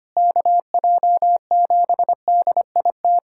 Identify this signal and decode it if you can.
Morse code, heard as a keyed tone, about 25 wpm, 700 Hz. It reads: KJ7DIT
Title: USA Callsign DE Callsign - 6 Characters - 25wpm